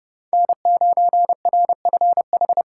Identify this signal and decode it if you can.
Morse code, heard as a keyed tone, about 35 wpm, 700 Hz. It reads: N9RFH
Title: Speed Racing - USA Call Signs 30wpm